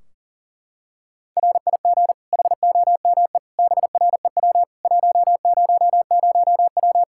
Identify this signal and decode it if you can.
Morse code, heard as a keyed tone, about 40 wpm, 700 Hz. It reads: RIG HOME BREW 100W
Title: Practice QSO - Exchange 2 - Info - 40wpm